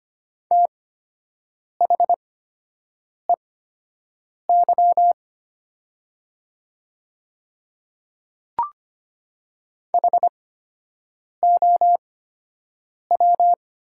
Morse code, heard as a keyed tone, about 25 wpm, spaced as for 7 wpm, 700 Hz. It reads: THEY HOW
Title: 8x Character Spacing - Top 100 Words - 25wpm